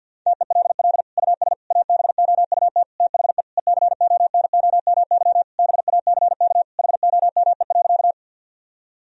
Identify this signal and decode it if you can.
Morse code, heard as a keyed tone, about 51 wpm, 700 Hz. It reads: TELL US ABOUT THE ECONOMY BACK HOME.
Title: Sentences from Top 1000 Words - Review Part 3 - 50wpm